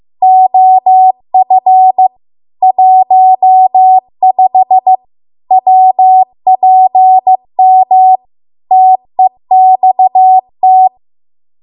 Morse code, heard as a keyed tone, about 15 wpm, 750 Hz. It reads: OF 15 WPM TEXT